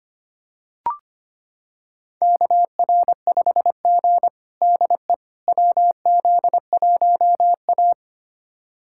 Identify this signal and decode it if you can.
Morse code, heard as a keyed tone, about 25 wpm, 700 Hz. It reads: KR5G DE WZ1A
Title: USA Callsign DE Callsign - 4 Characters - 25wpm